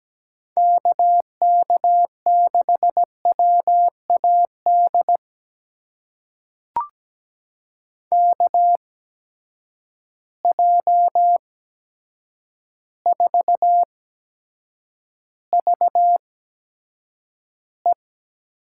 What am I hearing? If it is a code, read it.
Morse code, heard as a keyed tone, about 17 wpm, spaced as for 8 wpm, 700 Hz. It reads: KK6WAD KJ4VE
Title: USA Call Signs - 6 Characters - 8x Character Spacing - 17wpm